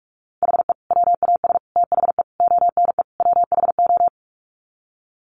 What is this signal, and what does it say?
Morse code, heard as a keyed tone, about 44 wpm, 700 Hz. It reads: HE WAS THE ONE WHO
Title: Sets of 5 Words Encore2 45wpm